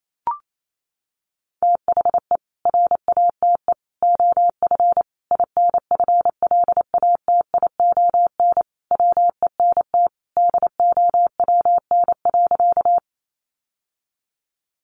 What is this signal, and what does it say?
Morse code, heard as a keyed tone, about 28 wpm, 700 Hz. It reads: THE RATE OF INFLATION WENT DOWN.
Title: Sentences from Top 1100 Words 28wpm